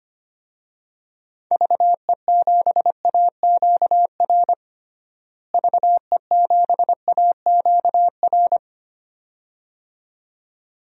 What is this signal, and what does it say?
Morse code, heard as a keyed tone, about 25 wpm, 700 Hz. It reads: VE7AQR VE7AQR
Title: DXCC-100 Call Signs - 6 Characters - Repeated 2x - 25wpm